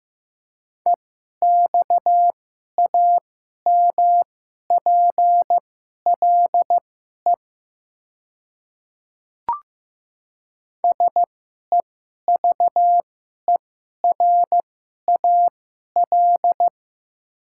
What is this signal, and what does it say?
Morse code, heard as a keyed tone, about 15 wpm, spaced as for 11 wpm, 700 Hz. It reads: EXAMPLE SEVERAL
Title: 2x Character Spacing - Top 500 Words - 15wpm